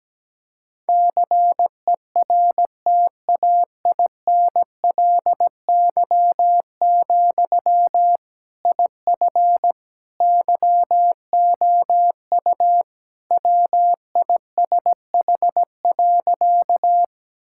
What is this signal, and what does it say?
Morse code, heard as a keyed tone, about 17 wpm, 700 Hz. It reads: CERTAINLY, IF YOU WISH.
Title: Sentences from Top 1600 Words 17wpm